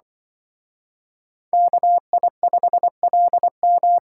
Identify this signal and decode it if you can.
Morse code, heard as a keyed tone, about 25 wpm, 700 Hz. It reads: KI5LM
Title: Speed Racing - USA 2x2 Call Signs 20wpm